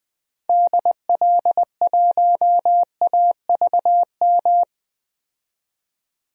Morse code, heard as a keyed tone, about 20 wpm, 700 Hz. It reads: DL1AVM
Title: DXCC-100 Call Signs - 6 Characters - 20wpm